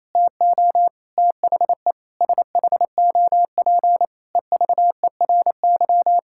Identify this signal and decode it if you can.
Morse code, heard as a keyed tone, about 28 wpm, 700 Hz. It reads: TO THE SHOP EVERY
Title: Sentences from Top 1600 Words 28wpm